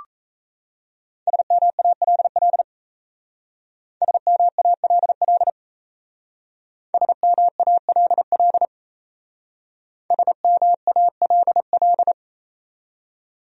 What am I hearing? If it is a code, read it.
Morse code, heard as a keyed tone, about 35 wpm, 700 Hz. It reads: SMALL SMALL SMALL SMALL
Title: Speed-Racing Top 300 Words 28wpm